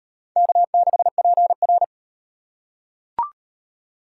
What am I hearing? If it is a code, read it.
Morse code, heard as a keyed tone, about 34 wpm, 700 Hz. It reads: K6PR